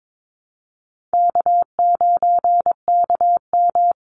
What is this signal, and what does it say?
Morse code, heard as a keyed tone, about 22 wpm, 700 Hz. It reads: K9KM
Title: Easy Call Signs Repeated 2x - 22wpm